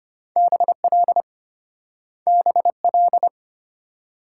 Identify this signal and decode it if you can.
Morse code, heard as a keyed tone, about 33 wpm, 700 Hz. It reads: BL BL